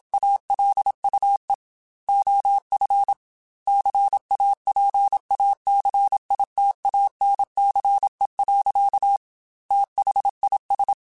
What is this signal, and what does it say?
Morse code, heard as a keyed tone, about 27 wpm, 800 Hz. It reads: ALUE OF CAPACITANCE. THIS
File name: Technical Subjects / Technical Subjects-mp3s / XC @25